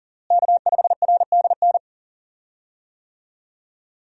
Morse code, heard as a keyed tone, about 40 wpm, 700 Hz. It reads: K5RDN
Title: USA Call Signs - 5 Characters - Repeated 2x - 40wpm